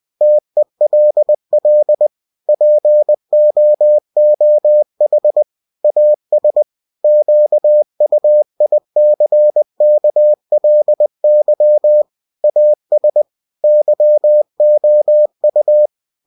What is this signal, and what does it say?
Morse code, heard as a keyed tone, about 20 wpm, 600 Hz. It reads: TELL POOH AS QUICKLY AS YOU
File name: cw-books / winnie-the-pooh / winnie-20wpm-page-0102